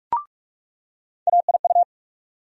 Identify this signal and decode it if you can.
Morse code, heard as a keyed tone, about 44 wpm, 700 Hz. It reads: AIU
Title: USA Call Sign Postfixes - 3 Letters 45wpm